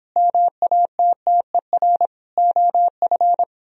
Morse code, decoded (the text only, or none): MATTER OF